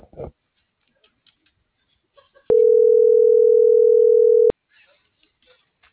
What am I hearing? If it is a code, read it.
Ringback tone (North America), 440 + 480 Hz.
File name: CENTRAL MORENA / out dulce.morena.2024